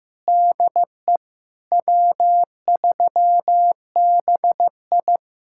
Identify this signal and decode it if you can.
Morse code, heard as a keyed tone, about 15 wpm, 700 Hz. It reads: DE W3BI
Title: Speed-Racing Common QSO Phrases Encore - 15wpm